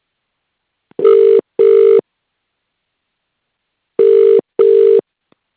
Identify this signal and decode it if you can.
Ringback tone (UK), 400 + 450 Hz.